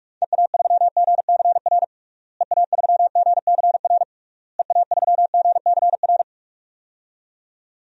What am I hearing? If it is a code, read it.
Morse code, heard as a keyed tone, about 41 wpm, 700 Hz. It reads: EA3GCR EA3GCR EA3GCR
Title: DXCC-100 Call Signs - 6 Characters - Repeated 3x - 45wpm